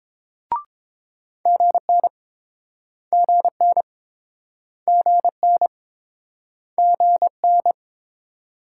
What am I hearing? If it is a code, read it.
Morse code, heard as a keyed tone, about 26 wpm, 700 Hz. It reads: GN GN GN GN
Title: Speed-Racing Multi-letter Phonemes - 22wpm